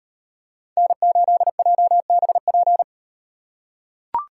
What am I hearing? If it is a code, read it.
Morse code, heard as a keyed tone, about 38 wpm, 700 Hz. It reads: N8JBP